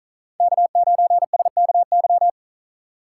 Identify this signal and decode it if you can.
Morse code, heard as a keyed tone, about 46 wpm, 700 Hz. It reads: K9SKY